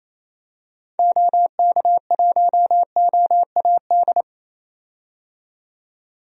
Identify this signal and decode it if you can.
Morse code, heard as a keyed tone, about 28 wpm, 700 Hz. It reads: OK1OAD
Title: DXCC-100 Call Signs - 6 Characters - 28wpm